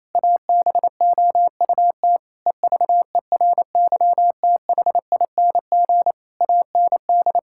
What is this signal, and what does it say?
Morse code, heard as a keyed tone, about 28 wpm, 700 Hz. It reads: ABOUT EVERYTHING AND
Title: Sentences from Top 1600 Words 28wpm